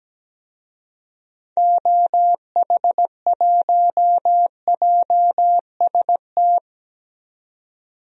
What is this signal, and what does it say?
Morse code, heard as a keyed tone, about 17 wpm, 700 Hz. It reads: OH1JST
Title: DXCC-100 Call Signs - 6 Characters - 17wpm